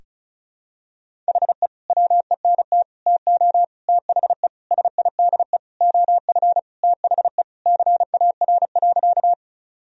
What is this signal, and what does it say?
Morse code, heard as a keyed tone, about 36 wpm, 700 Hz. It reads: HE WENT TO THE SIDE OF THE CAR.
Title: Sentences from Top 1000 Words - Review Part 1 - 35wpm